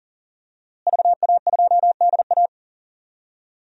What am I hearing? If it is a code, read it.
Morse code, heard as a keyed tone, about 40 wpm, 700 Hz. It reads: VA2DA
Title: DXCC-100 Call Signs - 5 Characters - Repeated 3x - 40wpm